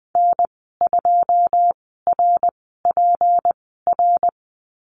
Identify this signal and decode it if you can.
Morse code, heard as a keyed tone, about 20 wpm, spaced as for 14 wpm, 700 Hz. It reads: N2RPR
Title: 167 Lesson 27g - Number 7 - Call Signs 20wpm